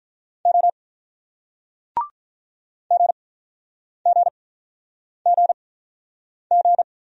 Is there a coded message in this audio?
Yes: Morse code from 0.5 to 6.8 s, heard as a keyed tone, about 48 wpm, spaced as for 9 wpm, 700 Hz. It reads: O GGGG